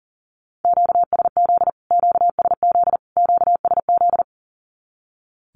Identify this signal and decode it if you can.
Morse code, heard as a keyed tone, about 40 wpm, 700 Hz. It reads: QSZ QSZ QSZ
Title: Q-Codes Repeated 3x 40wpm